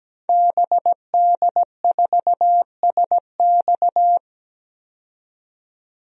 Morse code, heard as a keyed tone, about 17 wpm, 700 Hz. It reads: BD4SX
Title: DXCC-100 Call Signs - 5 Characters - 17wpm